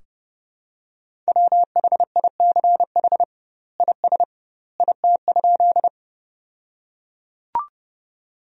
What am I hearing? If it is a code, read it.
Morse code, heard as a keyed tone, about 30 wpm, 700 Hz. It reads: WHICH IS IT?
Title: 2x Word-Spacing - Sentences from Top 100 Words - 30wpm